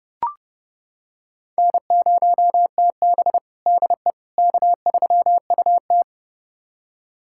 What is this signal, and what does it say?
Morse code, heard as a keyed tone, about 32 wpm, 700 Hz. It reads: N0TB DE K3UT
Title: USA Callsign DE Callsign - 4 Characters - 30wpm